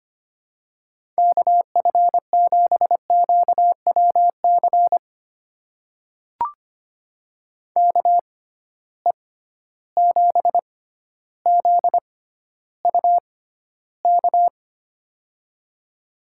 Morse code, heard as a keyed tone, about 25 wpm, spaced as for 13 wpm, 700 Hz. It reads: KF7QWC KE7ZUK
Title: USA Call Signs - 6 Characters - 6x Character Spacing - 25wpm